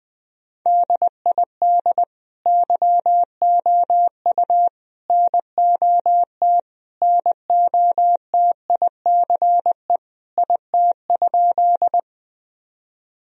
Morse code, heard as a keyed tone, about 20 wpm, 700 Hz. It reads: DID YOU NOT NOTICE IT?
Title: Sentences from Top 1000 Words - Review Part 2 - 20wpm